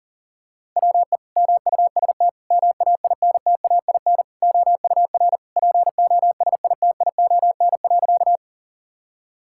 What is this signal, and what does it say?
Morse code, heard as a keyed tone, about 40 wpm, 700 Hz. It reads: WE MUST MAINTAIN OUR POSITION.
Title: Sentences from Top 1000 Words - Review Part 5 - 40wpm